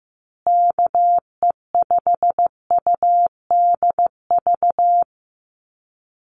Morse code, heard as a keyed tone, about 15 wpm, 700 Hz. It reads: KE5UDV
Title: USA Call Signs Repeated 2x 15wpm